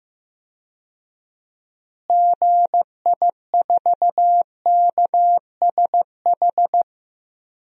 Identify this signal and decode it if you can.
Morse code, heard as a keyed tone, about 16 wpm, 700 Hz. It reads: GI4KSH
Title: DXCC-100 Call Signs - 6 Characters - Repeated 2x - 15wpm